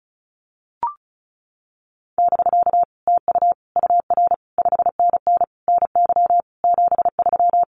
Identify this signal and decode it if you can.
Morse code, heard as a keyed tone, about 34 wpm, 700 Hz. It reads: <BK> TU UR 5NN NY 73
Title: Common QSO Phrases Encore 35wpm